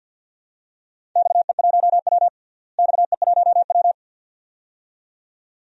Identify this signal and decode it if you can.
Morse code, heard as a keyed tone, about 52 wpm, 700 Hz. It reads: XE1W XE1W
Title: DXCC-100 Call Signs - 4 Characters - Repeated 2x - 50wpm